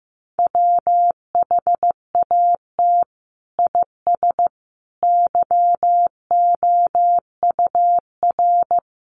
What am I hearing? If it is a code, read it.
Morse code, heard as a keyed tone, about 14 wpm, 700 Hz. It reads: WHAT IS YOUR
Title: Sentences from Top 1000 Words 15wpm